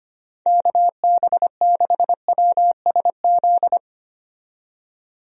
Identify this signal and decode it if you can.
Morse code, heard as a keyed tone, about 25 wpm, 700 Hz. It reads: KB6WSZ
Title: USA Call Signs - 6 Characters - 25wpm